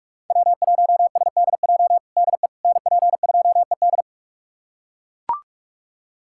Morse code, heard as a keyed tone, about 42 wpm, 700 Hz. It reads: W1SDJ DE NP2ED
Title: USA Callsign DE Callsign - 5 Characters - 45wpm